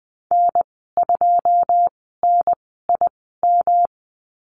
Morse code, heard as a keyed tone, about 20 wpm, spaced as for 15 wpm, 700 Hz. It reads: N2NIM
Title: 173 Lesson 28g - Number 9 - Call Signs 20wpm